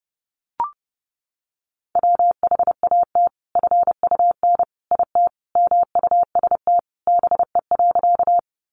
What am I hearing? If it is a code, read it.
Morse code, heard as a keyed tone, about 30 wpm, 700 Hz. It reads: WHAT FUN IT MUST BE.
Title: Sentences from Top 700 Words 30wpm